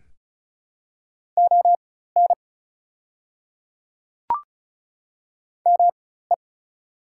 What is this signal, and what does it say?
Morse code, heard as a keyed tone, about 35 wpm, spaced as for 16 wpm, 700 Hz. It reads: ON ME
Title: 4x Character Spacing - Top 200 Words - 35wpm